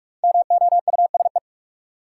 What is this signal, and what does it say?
Morse code, heard as a keyed tone, about 46 wpm, 700 Hz. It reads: MOUSE
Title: Sentences from Top 700 Words - New Word Drill - 45wpm